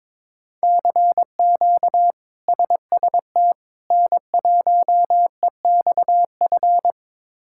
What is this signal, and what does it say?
Morse code, heard as a keyed tone, about 22 wpm, 700 Hz. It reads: CQ SST N1EXF
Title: K1USN SST QSO Practice - 22wpm